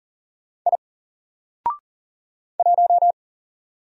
Morse code, heard as a keyed tone, about 39 wpm, 700 Hz. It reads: I 1
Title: Single Letter-Number Alt 40wpm